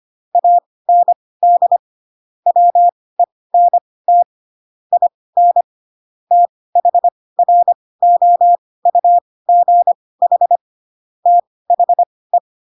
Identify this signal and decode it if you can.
Morse code, heard as a keyed tone, about 25 wpm, spaced as for 18 wpm, 700 Hz. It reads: AND WENT IN THROUGH THE
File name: Pinocchio / Pinocchio ch7 12 25 ik1hge